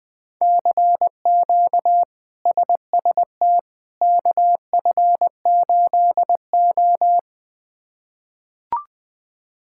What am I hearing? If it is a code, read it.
Morse code, heard as a keyed tone, about 20 wpm, 700 Hz. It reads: CQ SST KF8O
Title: K1USN SST QSO Practice - 20wpm